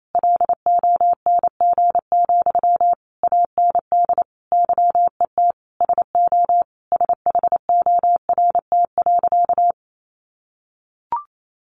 Morse code, heard as a keyed tone, about 28 wpm, 700 Hz. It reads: LONG, AND YET SO SHORT.
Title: Sentences from Top 500 Words 28wpm